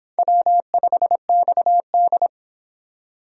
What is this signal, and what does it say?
Morse code, heard as a keyed tone, about 25 wpm, 700 Hz. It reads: W5XD